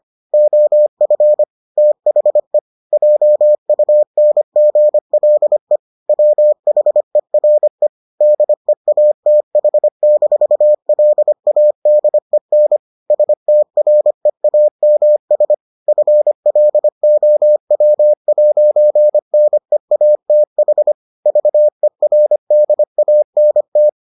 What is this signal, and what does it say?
Morse code, heard as a keyed tone, about 25 wpm, 600 Hz. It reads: OF THE JUNGLE WHERE DEATH-LADEN STREAMS FLOW 'NEATH VERDANT